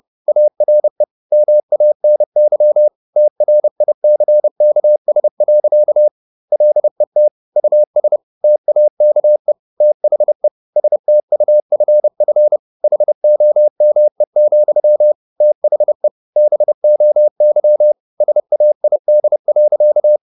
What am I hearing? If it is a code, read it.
Morse code, heard as a keyed tone, about 30 wpm, 600 Hz. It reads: ARE MANY TRICKS. LET US TAKE THE STUFF HOME, THE BOY SAID.
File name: the old man and the sea / the old man and the sea 1 30wpm